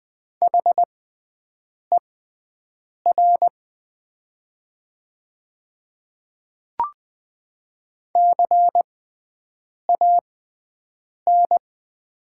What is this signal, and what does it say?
Morse code, heard as a keyed tone, about 20 wpm, spaced as for 7 wpm, 690 Hz. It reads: HER CAN
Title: 6x Character Spacing - Top 100 Words - 20wpm